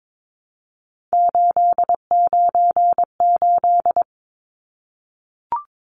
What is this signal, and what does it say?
Morse code, heard as a keyed tone, about 22 wpm, 700 Hz. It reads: 898
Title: Numbers - 3-Digit 22wpm